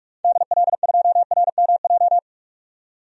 Morse code, heard as a keyed tone, about 44 wpm, 700 Hz. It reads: DL2RMJ
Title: DXCC-100 Call Signs - 6 Characters - 45wpm